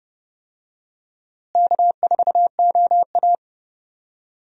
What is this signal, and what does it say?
Morse code, heard as a keyed tone, about 30 wpm, 700 Hz. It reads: K4OA
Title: USA Call Signs - 4 Characters - Repeated 2x - 30wpm